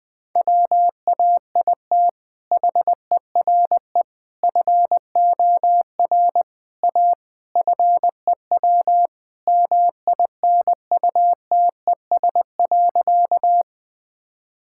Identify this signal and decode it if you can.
Morse code, heard as a keyed tone, about 21 wpm, 700 Hz. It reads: WAIT HERE FOR A FEW MINUTES.
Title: Sentences with 3-Word Phrases - 20wpm